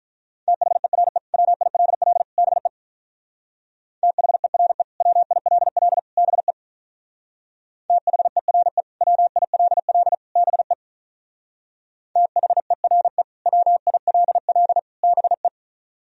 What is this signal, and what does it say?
Morse code, heard as a keyed tone, about 45 wpm, 700 Hz. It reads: THERE WILL BE THERE WILL BE THERE WILL BE THERE WILL BE